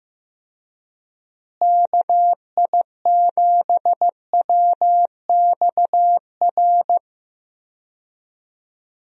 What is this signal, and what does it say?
Morse code, heard as a keyed tone, about 15 wpm, 700 Hz. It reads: KI7WXR
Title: USA Call Signs - 6 Characters - Repeated 3x - 15wpm